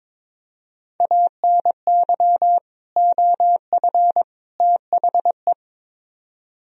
Morse code, heard as a keyed tone, about 22 wpm, 700 Hz. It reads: ANY OF THE
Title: Speed-Racing - Sets of 3 Words 22wpm